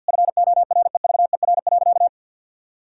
Morse code, heard as a keyed tone, about 50 wpm, 700 Hz. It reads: FOREVER.
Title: Sentences from Top 1600 Words 50wpm